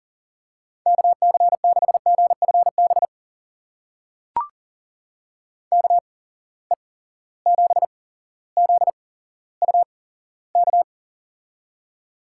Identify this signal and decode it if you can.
Morse code, heard as a keyed tone, about 40 wpm, spaced as for 17 wpm, 700 Hz. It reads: KC6GFB KE7ZUK
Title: USA Call Signs - 6 Characters - 8x Character Spacing - 40wpm